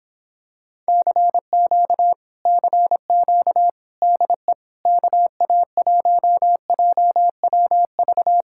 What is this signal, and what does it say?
Morse code, heard as a keyed tone, about 26 wpm, 700 Hz. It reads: CQ CQ DE KA1JWV